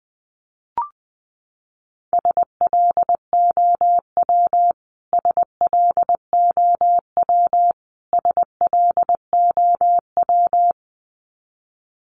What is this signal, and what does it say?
Morse code, heard as a keyed tone, about 20 wpm, 700 Hz. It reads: SLOW SLOW SLOW
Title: Top 500 Words Repeated 20wpm